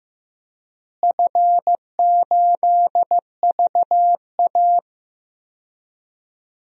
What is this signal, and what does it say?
Morse code, heard as a keyed tone, about 15 wpm, 700 Hz. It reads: F8VA
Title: DXCC-100 Call Signs - 4 Characters - Repeated 2x - 15wpm